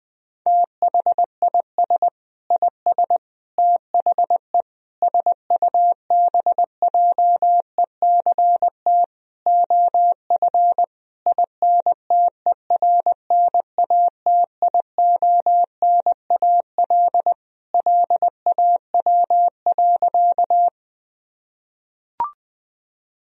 Morse code, heard as a keyed tone, about 19 wpm, 700 Hz. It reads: THIS IS THE SUBJECT OF INTERNATIONAL LAW.
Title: Sentences from Top 1000 Words - Review Part 4 - 20wpm